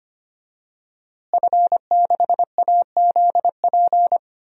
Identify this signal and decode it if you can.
Morse code, heard as a keyed tone, about 25 wpm, 700 Hz. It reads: F6AZP
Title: DXCC-100 Call Signs - 5 Characters - Repeated 2x - 25wpm